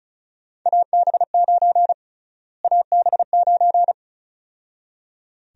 Morse code, heard as a keyed tone, about 36 wpm, 700 Hz. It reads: AB9 AB9
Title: USA Call Sign Prefixes - Repeated 2x - 35wpm